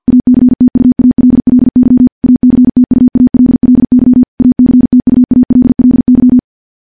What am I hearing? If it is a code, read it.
Morse code, heard as a keyed tone, about 50 wpm, 250 Hz. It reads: ACTUALLY ACTUALLY ACTUALLY
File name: morse / morse-training-50WPM